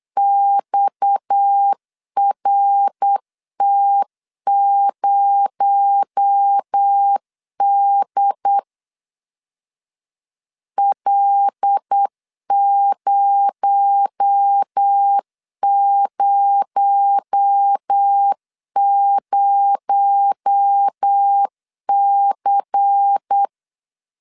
Morse code, heard as a keyed tone, about 8 wpm, 800 Hz. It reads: XRT0D L000C
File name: LEZIONI / 6 PKX0 / 6 PKX0 8wpm 3341